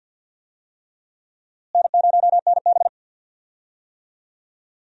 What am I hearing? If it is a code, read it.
Morse code, heard as a keyed tone, about 52 wpm, 700 Hz. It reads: N0NB
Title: USA Call Signs - 4 Characters - Repeated 3x - 50wpm